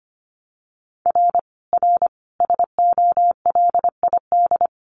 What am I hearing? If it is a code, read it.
Morse code, heard as a keyed tone, about 25 wpm, 700 Hz. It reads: R R SOLID
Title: Common QSO Phrases 25wpm